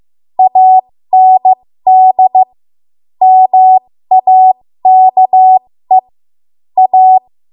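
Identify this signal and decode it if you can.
Morse code, heard as a keyed tone, about 15 wpm, spaced as for 13 wpm, 750 Hz. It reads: AND MAKE A
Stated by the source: ARRL Morse code practice at 13 WPM.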